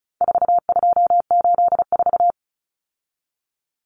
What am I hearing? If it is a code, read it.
Morse code, heard as a keyed tone, about 34 wpm, 700 Hz. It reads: 4284
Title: Common Cut Numbers - 4-Digit 35wpm